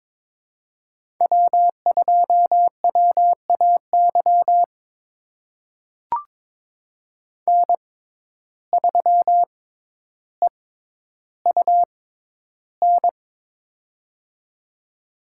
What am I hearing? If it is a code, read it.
Morse code, heard as a keyed tone, about 22 wpm, spaced as for 11 wpm, 700 Hz. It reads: W2WAY N3EUN
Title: USA Call Signs - 5 Characters - 6x Character Spacing - 22wpm